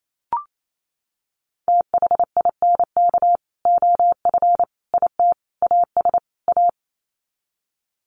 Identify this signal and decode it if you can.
Morse code, heard as a keyed tone, about 28 wpm, 700 Hz. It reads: THINK OF IT AS A
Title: Sets of 5 Words Encore2 28wpm